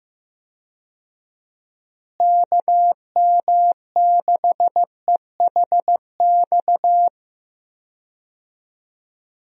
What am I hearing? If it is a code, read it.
Morse code, heard as a keyed tone, about 15 wpm, 700 Hz. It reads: KM6EHX
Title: USA Call Signs - 6 Characters - Repeated 2x - 15wpm